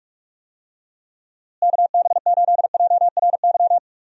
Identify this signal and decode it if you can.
Morse code, heard as a keyed tone, about 42 wpm, 700 Hz. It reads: KB8JRY
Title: USA Call Signs - 6 Characters - Repeated 2x - 45wpm